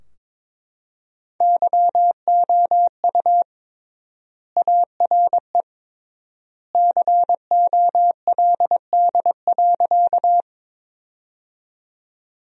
Morse code, heard as a keyed tone, about 21 wpm, 700 Hz. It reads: YOU ARE COLD.
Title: 3x Word-Spacing - Sentences from Top 400 Words - 22wpm